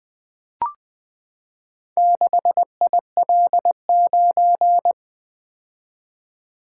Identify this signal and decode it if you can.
Morse code, heard as a keyed tone, about 20 wpm, 700 Hz. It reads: 6IL9
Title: Four Letter-Number Combinations - 20wpm